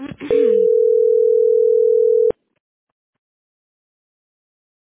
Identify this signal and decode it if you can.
Ringback tone (North America), 440 + 480 Hz.